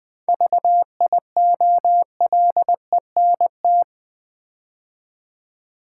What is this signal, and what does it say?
Morse code, heard as a keyed tone, about 20 wpm, 700 Hz. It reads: VIOLENT
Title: 7 Letter Words - First 3 Letters Given - 20wpm